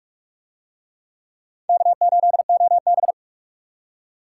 Morse code, heard as a keyed tone, about 44 wpm, 700 Hz. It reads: K8OB